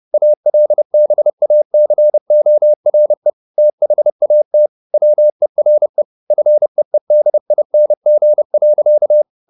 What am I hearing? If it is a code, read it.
Morse code, heard as a keyed tone, about 30 wpm, 600 Hz. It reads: ALBACORE THAT WERE FEEDING.
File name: the old man and the sea / the old man and the sea 1 30wpm